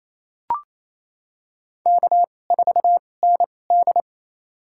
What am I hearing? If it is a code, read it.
Morse code, heard as a keyed tone, about 28 wpm, spaced as for 20 wpm, 700 Hz. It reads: K4ND
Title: USA Call Signs - 4 Characters - 2x Character Spacing - 28wpm